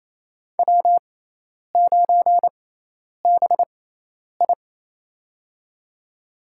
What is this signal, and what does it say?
Morse code, heard as a keyed tone, about 28 wpm, spaced as for 10 wpm, 700 Hz. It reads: W9BI
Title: USA Call Signs - 6x Character Spacing - 28wpm